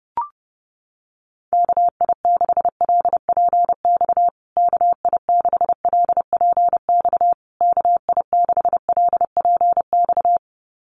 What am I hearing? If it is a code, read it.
Morse code, heard as a keyed tone, about 30 wpm, 700 Hz. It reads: KI6LPX KI6LPX KI6LPX
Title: USA Call Signs Repeated 3x 30wpm